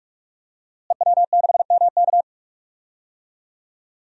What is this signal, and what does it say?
Morse code, heard as a keyed tone, about 42 wpm, 700 Hz. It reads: EW6MK
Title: DXCC-100 Call Signs - 5 Characters - Repeated 3x - 45wpm